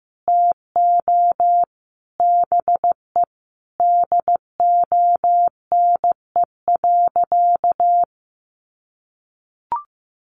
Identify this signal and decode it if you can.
Morse code, heard as a keyed tone, about 15 wpm, 700 Hz. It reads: TO BE DONE.
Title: Sentences from Top 800 Words 15wpm